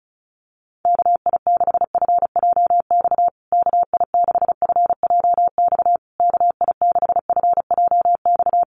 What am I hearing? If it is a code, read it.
Morse code, heard as a keyed tone, about 34 wpm, 700 Hz. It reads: KI6FJX KI6FJX KI6FJX
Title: USA Call Signs Repeated 3x 35wpm